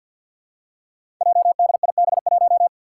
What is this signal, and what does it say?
Morse code, heard as a keyed tone, about 45 wpm, 700 Hz. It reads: JDIB1
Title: Five Challenging Letter-Number Combinations 50wpm